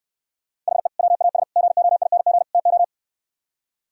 Morse code, heard as a keyed tone, about 68 wpm, 700 Hz. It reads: HE WAS GOING TO